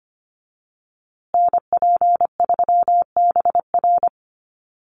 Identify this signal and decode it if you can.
Morse code, heard as a keyed tone, about 25 wpm, 700 Hz. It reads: NP3BR
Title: USA Call Signs Repeated 2x 25wpm